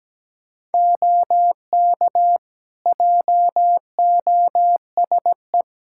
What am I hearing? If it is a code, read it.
Morse code, heard as a keyed tone, about 17 wpm, 700 Hz. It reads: OK JOSE
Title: K1USN SST QSO Practice 17wpm